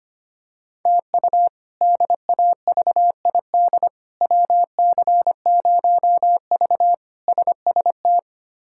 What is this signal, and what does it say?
Morse code, heard as a keyed tone, about 25 wpm, 700 Hz. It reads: TU DAVID WC0V SST
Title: K1USN SST QSO Practice - 25wpm